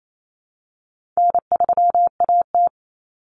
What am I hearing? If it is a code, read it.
Morse code, heard as a keyed tone, about 28 wpm, 700 Hz. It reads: N3AT
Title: Common Cut Numbers - 4-Digit 28wpm